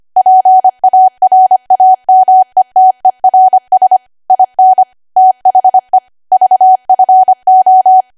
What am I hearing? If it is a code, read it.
Morse code, heard as a keyed tone, about 25 wpm, 750 Hz. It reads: PARAMETERS IN THE VFO
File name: Archive / 25 WPM / 25WPM